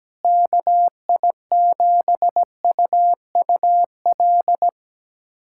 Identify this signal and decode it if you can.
Morse code, heard as a keyed tone, about 17 wpm, 700 Hz. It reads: KI7UUL
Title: USA Callsign DE Callsign 17wpm